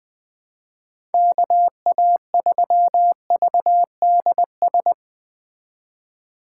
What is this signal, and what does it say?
Morse code, heard as a keyed tone, about 20 wpm, 700 Hz. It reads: KA3VDS
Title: USA Call Signs - 6 Characters - 2x Character Spacing - 20wpm